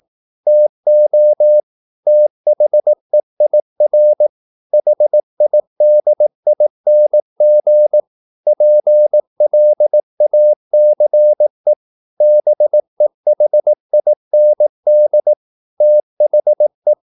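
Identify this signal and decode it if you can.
Morse code, heard as a keyed tone, about 18 wpm, 600 Hz. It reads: TO THEIR HIDING PLACE BEHIND THE
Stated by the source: Morse practice at 18 wpm, English text, from CW med Gnister.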